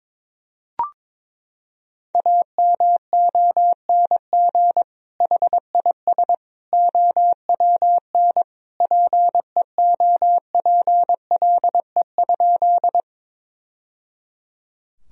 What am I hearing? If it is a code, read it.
Morse code, heard as a keyed tone, about 22 wpm, 700 Hz. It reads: AMONG HIS OWN PEOPLE?
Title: Sentences from Top 1000 Words - Review Part 3 - 22wpm